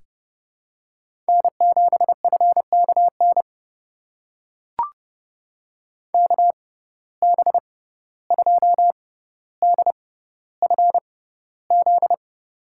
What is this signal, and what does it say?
Morse code, heard as a keyed tone, about 30 wpm, spaced as for 15 wpm, 700 Hz. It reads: N7FKN KB2DFZ
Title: USA Call Signs - 6x Character Spacing - 30wpm